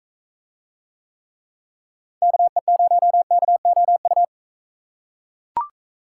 Morse code, heard as a keyed tone, about 39 wpm, 700 Hz. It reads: KE0KOU